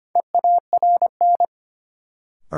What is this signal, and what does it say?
Morse code, heard as a keyed tone, about 25 wpm, 700 Hz. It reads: EARN
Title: Sentences from Top 1500 Words - New Word Drill Alt - 25wpm